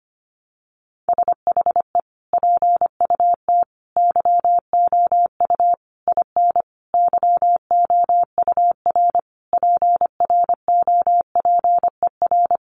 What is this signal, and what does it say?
Morse code, heard as a keyed tone, about 25 wpm, 700 Hz. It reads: SHE PUT YOU IN YOUR PROPER
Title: Sentences from Top 800 Words 25wpm